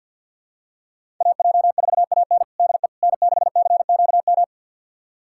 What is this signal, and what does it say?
Morse code, heard as a keyed tone, about 47 wpm, 700 Hz. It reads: AJ4AN DE N6CQM
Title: USA Callsign DE Callsign - 5 Characters - 50wpm